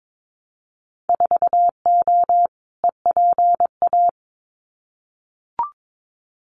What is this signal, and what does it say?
Morse code, heard as a keyed tone, about 22 wpm, 700 Hz. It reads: 4O EPA
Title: Winter Field Day Exchanges 22wpm